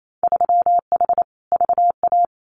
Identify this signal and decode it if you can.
Morse code, heard as a keyed tone, about 28 wpm, 700 Hz. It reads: 3H VA
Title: Winter Field Day Exchanges Encore 28wpm